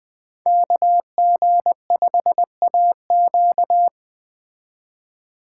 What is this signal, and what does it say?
Morse code, heard as a keyed tone, about 20 wpm, 700 Hz. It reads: KG5AQ